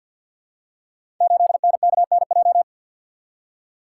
Morse code, heard as a keyed tone, about 52 wpm, 700 Hz. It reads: 8NKNJ
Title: Code Groups - 5 Letter:Numbers - 50wpm